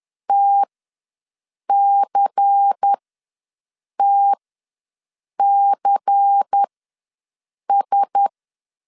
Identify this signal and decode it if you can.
Morse code, heard as a keyed tone, about 10 wpm, spaced as for 6 wpm, 800 Hz. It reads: TCTCS
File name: LEZIONI / 5 LBVC / 5 LBVC ws 10wpm 3733